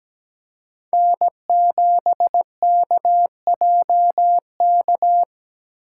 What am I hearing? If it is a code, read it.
Morse code, heard as a keyed tone, about 17 wpm, 700 Hz. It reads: N7KJK
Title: USA Call Signs - 5 Characters - 3x Character Spacing - 17wpm